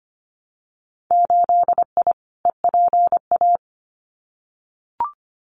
Morse code, heard as a keyed tone, about 25 wpm, 700 Hz. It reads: 8I EPA
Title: Winter Field Day Exchanges Encore 25wpm